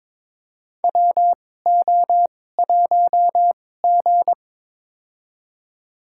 Morse code, heard as a keyed tone, about 22 wpm, spaced as for 16 wpm, 700 Hz. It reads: WO1G
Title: USA Call Signs - 4 Characters - 2x Character Spacing - 22wpm